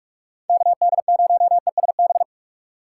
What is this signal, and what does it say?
Morse code, heard as a keyed tone, about 46 wpm, 700 Hz. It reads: KD0ESB
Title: USA Call Signs - 2x Character Spacing - 45wpm